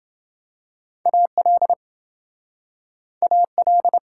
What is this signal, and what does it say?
Morse code, heard as a keyed tone, about 29 wpm, 700 Hz. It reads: AL AL